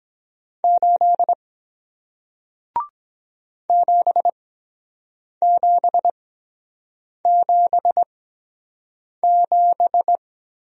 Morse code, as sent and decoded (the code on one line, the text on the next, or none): ---.. / --... --... --... --...
8 7777